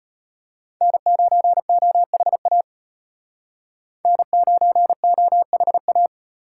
Morse code, heard as a keyed tone, about 32 wpm, 700 Hz. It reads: N9OHA N9OHA